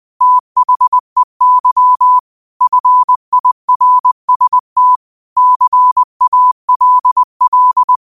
Morse code, heard as a keyed tone, about 20 wpm, 1000 Hz. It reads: THEY FIRST CALL
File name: qotdfiles / 20wpm